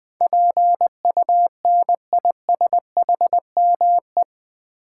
Morse code, heard as a keyed tone, about 20 wpm, 700 Hz. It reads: PUNISHME
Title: Build A Word - 4 Letters At A Time - 10-Letter Words - 20wpm